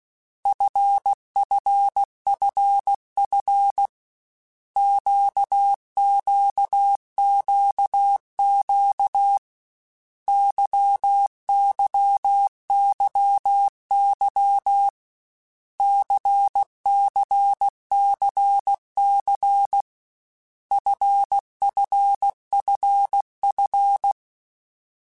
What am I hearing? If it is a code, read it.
Morse code, heard as a keyed tone, about 16 wpm, 800 Hz. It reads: FFFF QQQQ YYYY CCCC FFFF
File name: Basic Letter Groups By7Groups / LFCYQ / LFCYQ@13